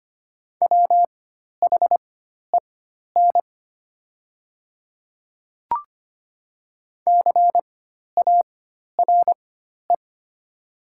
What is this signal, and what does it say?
Morse code, heard as a keyed tone, about 25 wpm, spaced as for 12 wpm, 700 Hz. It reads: WHEN CARE
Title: 4x Character Spacing - Top 300 Words - 25wpm